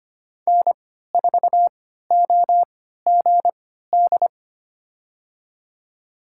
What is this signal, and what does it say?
Morse code, heard as a keyed tone, about 25 wpm, spaced as for 14 wpm, 700 Hz. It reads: N4OGD
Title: USA Call Signs - 5 Characters - 3x Character Spacing - 25wpm